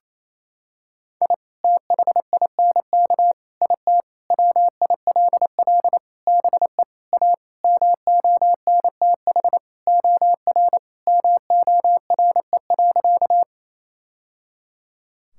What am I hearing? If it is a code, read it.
Morse code, heard as a keyed tone, about 28 wpm, 700 Hz. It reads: I THINK IT WILL BE A MONTH OR MORE.
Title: Sentences from Top 1000 Words - Review Part 5 - 28wpm